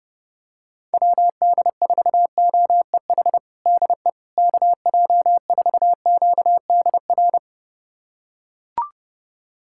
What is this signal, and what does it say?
Morse code, heard as a keyed tone, about 31 wpm, 700 Hz. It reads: WD4OEH DE KJ4QDR
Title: USA Callsign DE Callsign - 6 Characters - 30wpm